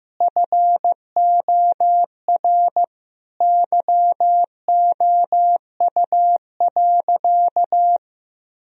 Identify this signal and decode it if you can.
Morse code, heard as a keyed tone, about 15 wpm, 700 Hz. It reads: FOR YOU.
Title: Sentences from Top 1000 Words - Review Part 3 - 15wpm